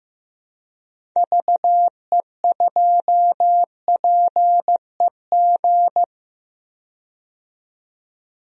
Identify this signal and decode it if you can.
Morse code, heard as a keyed tone, about 15 wpm, 700 Hz. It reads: VE2PEG
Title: DXCC-100 Call Signs - 6 Characters - 15wpm